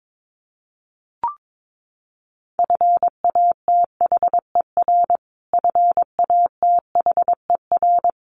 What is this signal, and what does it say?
Morse code, heard as a keyed tone, about 22 wpm, 700 Hz. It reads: FATHER FATHER
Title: Top 200 Words Repeated 22wpm